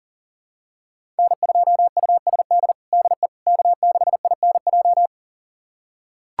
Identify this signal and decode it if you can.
Morse code, heard as a keyed tone, about 40 wpm, 700 Hz. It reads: N2USD DE K6INJ
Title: USA Callsign DE Callsign - 5 Characters - 40wpm